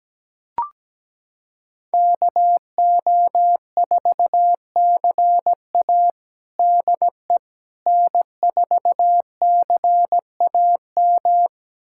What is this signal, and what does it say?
Morse code, heard as a keyed tone, about 17 wpm, 700 Hz. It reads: KO4CA DE N4CAM
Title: USA Callsign DE Callsign - 5 Characters - 17wpm